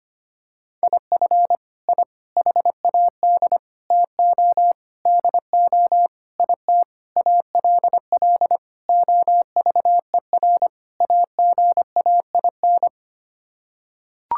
Morse code, heard as a keyed tone, about 25 wpm, 700 Hz. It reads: IF I HAD TO DO IT ALL OVER AGAIN
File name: Sets of 9 Words - 25wpm